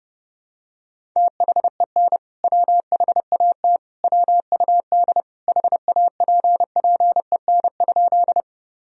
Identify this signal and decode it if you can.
Morse code, heard as a keyed tone, about 30 wpm, 700 Hz. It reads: THEN WHAT WUD HAPPEN?
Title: Sentences from Top 400 Words and Abbreviations 30wpm